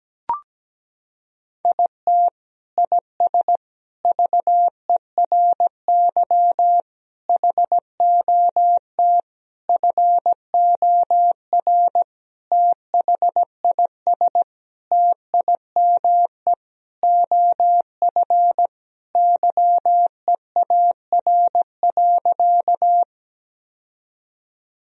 Morse code, heard as a keyed tone, about 17 wpm, 700 Hz. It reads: IT IS VERY HOT FOR THIS TIME OF YEAR.
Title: Sentences from Top 1000 Words - Review Part 3 - 17wpm